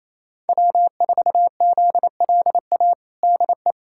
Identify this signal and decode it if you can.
Morse code, heard as a keyed tone, about 28 wpm, 700 Hz. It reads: W4ZLA DE
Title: USA Callsign DE Callsign 28wpm